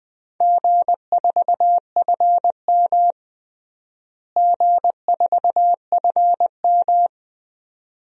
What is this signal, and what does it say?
Morse code, heard as a keyed tone, about 21 wpm, 700 Hz. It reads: G4FM G4FM
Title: DXCC-100 Call Signs - 4 Characters - Repeated 2x - 20wpm